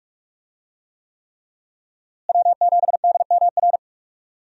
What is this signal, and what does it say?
Morse code, heard as a keyed tone, about 46 wpm, 700 Hz. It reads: W7DMR
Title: USA Call Signs - 5 Characters - Repeated 3x - 45wpm